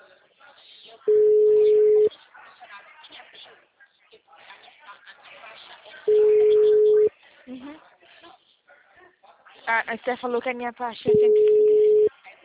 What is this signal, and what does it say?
Ringback tone (Europe, 425 Hz), with speech.